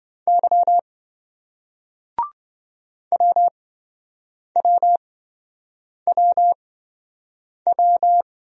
Morse code, heard as a keyed tone, about 28 wpm, spaced as for 7 wpm, 700 Hz. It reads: Y WWWW